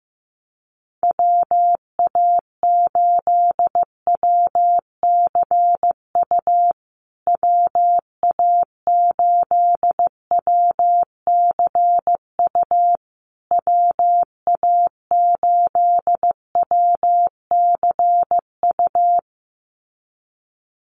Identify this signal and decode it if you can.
Morse code, heard as a keyed tone, about 15 wpm, 700 Hz. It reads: WA8WCU WA8WCU WA8WCU
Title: USA Call Signs Repeated 3x 15wpm